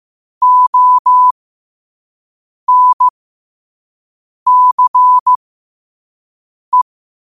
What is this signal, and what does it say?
Morse code, heard as a keyed tone, about 15 wpm, spaced as for 5 wpm, 1000 Hz. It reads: ONCE
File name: qotdfiles / 5wpm